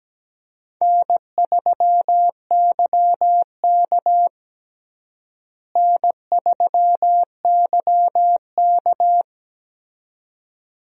Morse code, heard as a keyed tone, about 17 wpm, 700 Hz. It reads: N3YK N3YK
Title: USA Call Signs - 4 Characters - Repeated 2x - 17wpm